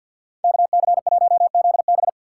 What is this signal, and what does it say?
Morse code, heard as a keyed tone, about 47 wpm, 700 Hz. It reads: KC1ZB
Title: USA Call Signs - 5 Characters - 3x Character Spacing - 50wpm